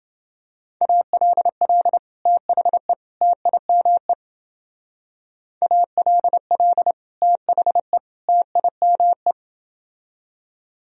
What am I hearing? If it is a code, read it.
Morse code, heard as a keyed tone, about 28 wpm, 700 Hz. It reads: ALL THE TIME ALL THE TIME